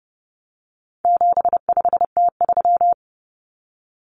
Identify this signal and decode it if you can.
Morse code, heard as a keyed tone, about 30 wpm, 700 Hz. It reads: 75T3
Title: Common Cut Numbers - 4-Digit 30wpm